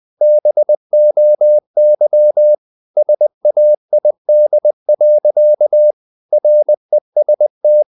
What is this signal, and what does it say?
Morse code, heard as a keyed tone, about 20 wpm, 600 Hz. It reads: BOY SAID. REST
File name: the old man and the sea part 2 20wpm